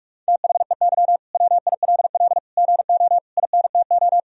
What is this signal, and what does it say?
Morse code, heard as a keyed tone, about 46 wpm, 700 Hz. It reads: THEY WILL GO INTO
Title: Sentences from Top 1000 Words - Review Part 4 - 45wpm